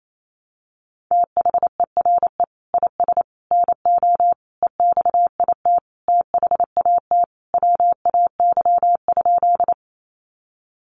Morse code, heard as a keyed tone, about 28 wpm, 700 Hz. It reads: THERE IS NO EXIT THAT WAY?
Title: Sentences from Top 1100 Words 28wpm